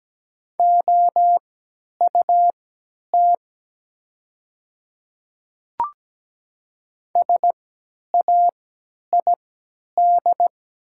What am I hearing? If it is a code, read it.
Morse code, heard as a keyed tone, about 17 wpm, spaced as for 10 wpm, 700 Hz. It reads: OUT SAID
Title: 3x Character Spacing - Top 100 Words - 17wpm